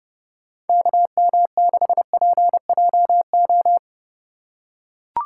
Morse code, heard as a keyed tone, about 31 wpm, 700 Hz. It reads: KM6PJO